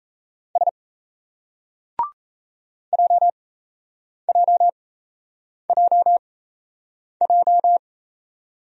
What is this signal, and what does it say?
Morse code, heard as a keyed tone, about 37 wpm, spaced as for 8 wpm, 700 Hz. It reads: S JJJJ